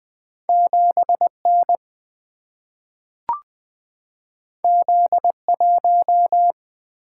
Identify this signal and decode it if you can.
Morse code, heard as a keyed tone, about 20 wpm, 700 Hz. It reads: 7N Z1
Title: Single-Letter Single-Number 20wpm